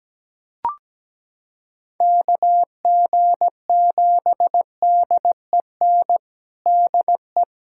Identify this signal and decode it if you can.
Morse code, heard as a keyed tone, about 17 wpm, 700 Hz. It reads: KG7DEN DE
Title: USA Callsign DE Callsign 17wpm